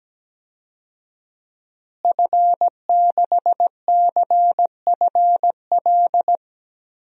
Morse code, heard as a keyed tone, about 17 wpm, 700 Hz. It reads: F6CFL
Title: DXCC-100 Call Signs - 5 Characters - Repeated 2x - 17wpm